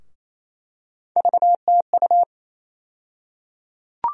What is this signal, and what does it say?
Morse code, heard as a keyed tone, about 28 wpm, 700 Hz. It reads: VTU
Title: USA Call Sign Postfixes - 3 Letters 28wpm